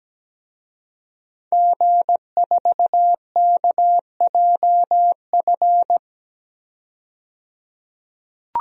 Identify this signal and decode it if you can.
Morse code, heard as a keyed tone, about 17 wpm, 700 Hz. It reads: G4KJF
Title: DXCC-100 Call Signs - 5 Characters - 17wpm